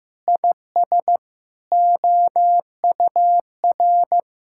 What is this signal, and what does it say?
Morse code, heard as a keyed tone, about 16 wpm, 700 Hz. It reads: IS OUR
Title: Sentences from Top 1000 Words - Review Part 6 - 15wpm